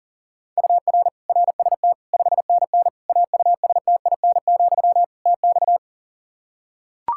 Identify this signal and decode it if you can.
Morse code, heard as a keyed tone, about 40 wpm, 700 Hz. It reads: UR RST 5NN AUSTIN, TX
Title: Practice QSO - Exchange 1 - Beginning - 40wpm